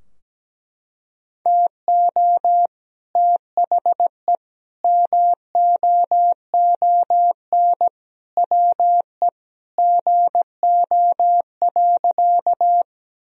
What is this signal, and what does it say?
Morse code, heard as a keyed tone, about 17 wpm, 700 Hz. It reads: TO THE MOON WE GO.
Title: Sentences from Top 1000 Words - Review Part 3 - 17wpm